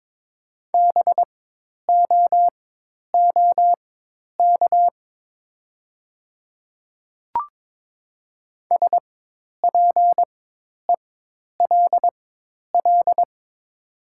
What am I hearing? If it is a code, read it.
Morse code, heard as a keyed tone, about 23 wpm, spaced as for 10 wpm, 690 Hz. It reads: BOOK SPELL
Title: 4x Character Spacing - Top 300 Words - 22wpm